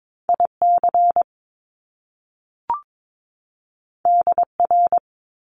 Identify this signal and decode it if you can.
Morse code, heard as a keyed tone, about 22 wpm, 700 Hz. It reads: IC DR
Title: Top Two Letter Combinations in Words 22wpm